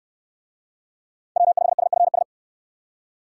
Morse code, heard as a keyed tone, about 68 wpm, 700 Hz. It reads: W5SFS